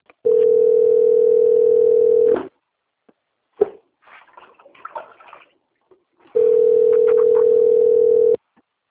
Ringback tone (North America), 440 + 480 Hz.
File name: out- - - - - .13132